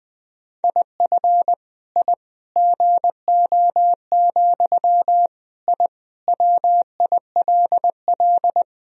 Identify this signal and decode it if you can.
Morse code, heard as a keyed tone, about 21 wpm, 700 Hz. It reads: IF I GO, I WILL
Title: Sentences from Top 1000 Words - Review Part 3 - 20wpm